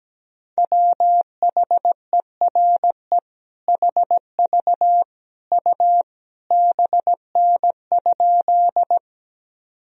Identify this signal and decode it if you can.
Morse code, heard as a keyed tone, about 17 wpm, 700 Hz. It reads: WHERE HV U BN?
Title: Sentences from Top 200 Words and Abbreviations 17wpm